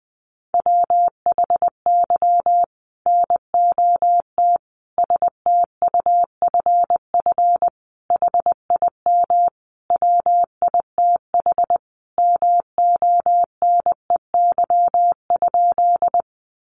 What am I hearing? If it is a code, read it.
Morse code, heard as a keyed tone, about 20 wpm, 700 Hz. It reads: WHY NOT STUFF HIM WITH MONEY?
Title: Sentences from Top 1400 Words 20wpm